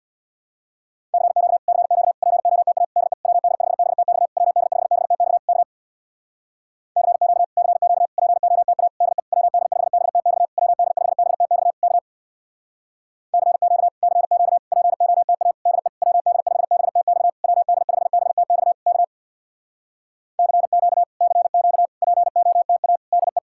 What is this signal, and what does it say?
Morse code, heard as a keyed tone, about 63 wpm, 700 Hz. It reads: CQ CQ POTA DE WD5BTX WD5BTX K CQ CQ POTA DE WD5BTX WD5BTX K CQ CQ POTA DE WD5BTX WD5BTX K CQ CQ POTA DE